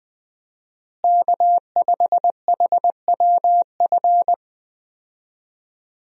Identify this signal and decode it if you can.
Morse code, heard as a keyed tone, about 20 wpm, 700 Hz. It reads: K5HWF
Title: USA Call Signs - 5 Characters - Repeated 3x - 20wpm